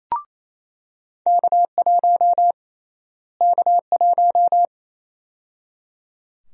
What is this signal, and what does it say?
Morse code, heard as a keyed tone, about 28 wpm, 700 Hz. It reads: K1 K1
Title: USA Call Sign Prefixes - Repeated 2x - 28wpm